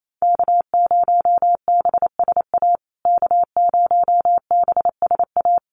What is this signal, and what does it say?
Morse code, heard as a keyed tone, about 28 wpm, 690 Hz. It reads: K0BSA K0BSA
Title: USA Call Signs Repeated 3x 28wpm